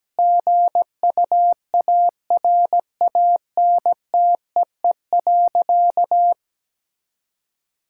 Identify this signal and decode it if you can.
Morse code, heard as a keyed tone, about 17 wpm, 700 Hz. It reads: GUARANTEE.
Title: Sentences from Top 1600 Words 17wpm